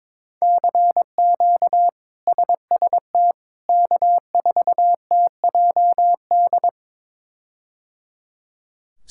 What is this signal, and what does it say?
Morse code, heard as a keyed tone, about 22 wpm, 700 Hz. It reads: CQ SST K4TJD
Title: K1USN SST QSO Practice - 22wpm